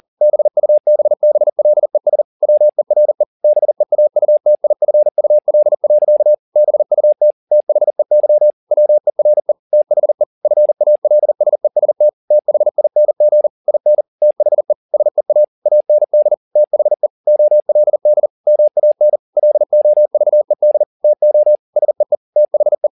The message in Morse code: BUBBLES WERE BEAUTIFUL. BUT THEY WERE THE FALSEST THING IN THE SEA AND THE OLD MAN LOVED TO SEE THE

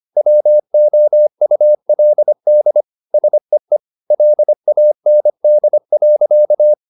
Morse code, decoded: WOULD SEE LAND.

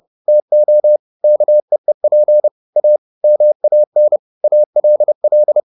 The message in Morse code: TO KEEP A MAN ALL

